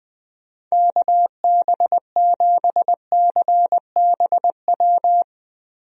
Morse code, heard as a keyed tone, about 20 wpm, 700 Hz. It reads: KB7CBW
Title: USA Call Signs - 6 Characters - Repeated 3x - 20wpm